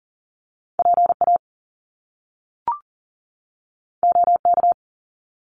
Morse code, heard as a keyed tone, about 39 wpm, 700 Hz. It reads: PA OK
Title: Top Two Letter Combinations in Words 40wpm